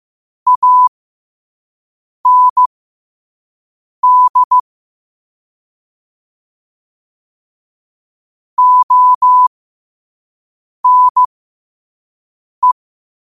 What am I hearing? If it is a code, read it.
Morse code, heard as a keyed tone, about 15 wpm, spaced as for 5 wpm, 1000 Hz. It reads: AND ONE
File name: qotdfiles / 5wpm